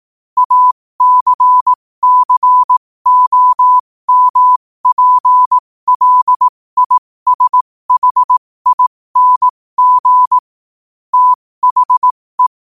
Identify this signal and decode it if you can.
Morse code, heard as a keyed tone, about 18 wpm, spaced as for 15 wpm, 1000 Hz. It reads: ACCOMPLISHING THE
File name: qotdfiles / 15wpm